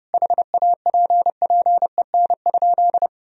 Morse code, heard as a keyed tone, about 30 wpm, 700 Hz. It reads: HAPPEN?
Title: Sentences from Top 400 Words and Abbreviations 30wpm